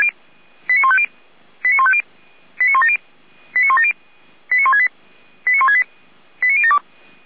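Five-tone selective call (ZVEI), heard as quick runs of five tones, tones of about 80 ms. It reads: 89150 89150 89170 89170 89168 89168 89072
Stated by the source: ZVEI five-tone call 89169 received on an SDR.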